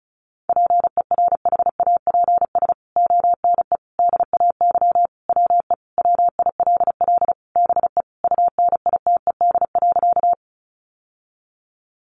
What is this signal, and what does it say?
Morse code, heard as a keyed tone, about 34 wpm, 700 Hz. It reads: PERHAPS ONE DAY WE WILL BE UNITED.
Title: Sentences from Top 600 Words 35wpm